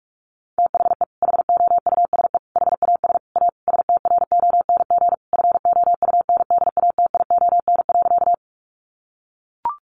Morse code, heard as a keyed tone, about 44 wpm, 700 Hz. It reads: THE HOUSE HAS A STRONG FOUNDATION.
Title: Sentences from Top 1200 Words 45wpm